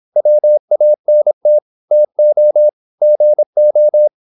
Morse code, heard as a keyed tone, about 26 wpm, 600 Hz. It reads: WANT TO GO
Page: Never 26wpm | CW med Gnister